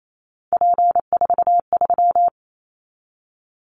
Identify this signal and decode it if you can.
Morse code, heard as a keyed tone, about 28 wpm, 700 Hz. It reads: P43
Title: Callsign Prefixes for Top 100 DXCC - 28wpm